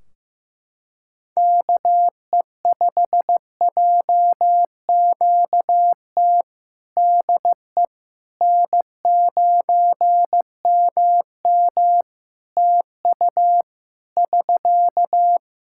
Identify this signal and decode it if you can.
Morse code, heard as a keyed tone, about 14 wpm, 700 Hz. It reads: KE5JQT DE N9MM TU <SK>
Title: K1USN SST QSO Practice 15wpm